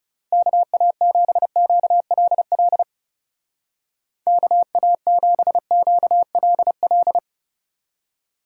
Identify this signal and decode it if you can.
Morse code, heard as a keyed tone, about 35 wpm, 700 Hz. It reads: KA7QLL KA7QLL
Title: Speed Racing - USA 2x3 Call Signs 30wpm